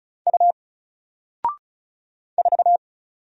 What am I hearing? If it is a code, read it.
Morse code, heard as a keyed tone, about 36 wpm, 700 Hz. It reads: U 4
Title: Single Character (Letter:Number:Punctuation:Prosign) Alt 35wpm